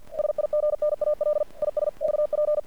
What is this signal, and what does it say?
Morse code, heard as a keyed tone, about 47 wpm, 600 Hz. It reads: SIGNAL IS VP
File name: archived / CRSnet